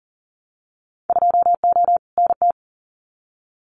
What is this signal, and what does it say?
Morse code, heard as a keyed tone, about 39 wpm, 700 Hz. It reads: 2O NT
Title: Winter Field Day Exchanges Encore 40wpm